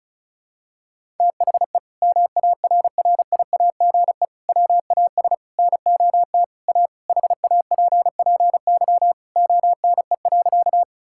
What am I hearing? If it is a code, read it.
Morse code, heard as a keyed tone, about 34 wpm, 700 Hz. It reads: THE MARRIAGE WAS NOT A HAPPY ONE.
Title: Sentences from Top 1000 Words - Review Part 5 - 35wpm